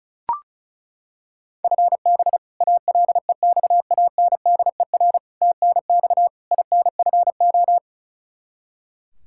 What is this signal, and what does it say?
Morse code, heard as a keyed tone, about 36 wpm, 700 Hz. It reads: FB ALEXANDER TNX INFO
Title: K1USN SST QSO Practice 35wpm